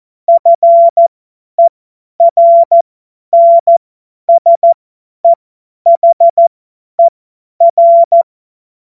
Morse code, heard as a keyed tone, about 14 wpm, spaced as for 10 wpm, 680 Hz. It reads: FERNSEHER
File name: Morsekurs / Witze / Witz 21. März 2023 50 F70